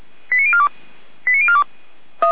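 Five-tone selective call (ZVEI), heard as quick runs of five tones, tones of about 70 ms. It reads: 89042 89042
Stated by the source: Five-tone ZVEI selective call received by SDR.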